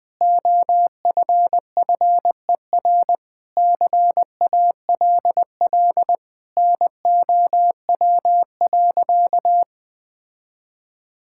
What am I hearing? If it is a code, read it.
Morse code, heard as a keyed tone, about 21 wpm, 700 Hz. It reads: OFFER CALL NOW.
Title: Sentences with 4-Word Phrases - 20wpm